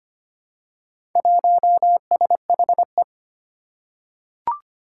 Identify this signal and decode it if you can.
Morse code, heard as a keyed tone, about 25 wpm, 700 Hz. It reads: 1SHE
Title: Four Challenging Letter-Number Combinations 25wpm